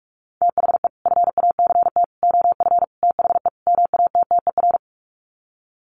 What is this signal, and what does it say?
Morse code, heard as a keyed tone, about 44 wpm, 700 Hz. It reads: THE FACT OF THE MATTER
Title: Sets of 5 Words 45wpm